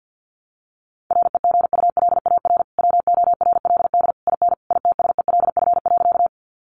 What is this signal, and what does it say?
Morse code, heard as a keyed tone, about 50 wpm, 700 Hz. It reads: REGULAR WORLD IN ITSELF.
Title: Sentences from Top 900 Words 50wpm